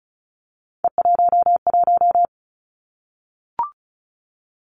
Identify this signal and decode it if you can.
Morse code, heard as a keyed tone, about 36 wpm, 700 Hz. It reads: E11
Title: Three Challenging Letter-Number Combinations 35wpm